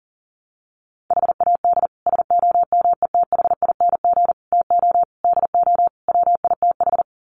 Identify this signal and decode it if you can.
Morse code, heard as a keyed tone, about 37 wpm, 700 Hz. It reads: HAD SOMETHING TO DO WITH
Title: Sets of 5 Words 40wpm